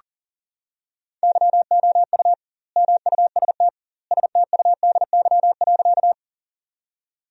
Morse code, heard as a keyed tone, about 37 wpm, 700 Hz. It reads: YOU MUST STUDY.
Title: 2x Word-Spacing - Sentences from Top 300 Words - 40wpm